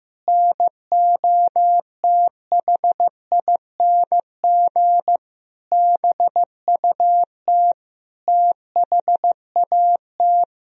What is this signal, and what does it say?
Morse code, heard as a keyed tone, about 16 wpm, 700 Hz. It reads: NOTHING BUT THAT
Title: Sentences from Top 1000 Words - Review Part 1 - 15wpm